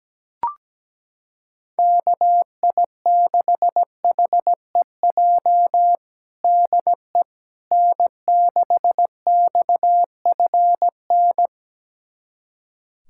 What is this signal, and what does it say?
Morse code, heard as a keyed tone, about 17 wpm, 700 Hz. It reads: KI6HEJ DE N6XFN
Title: USA Callsign DE Callsign 17wpm